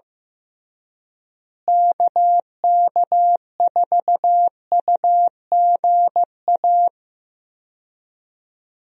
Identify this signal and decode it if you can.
Morse code, heard as a keyed tone, about 15 wpm, 700 Hz. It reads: KK4UGA
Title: USA Call Signs - 6 Characters - Repeated 3x - 15wpm